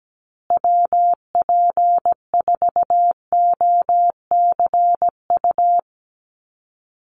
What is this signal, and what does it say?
Morse code, heard as a keyed tone, about 17 wpm, 700 Hz. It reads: WP4OCU
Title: USA Call Signs Repeated 3x 17wpm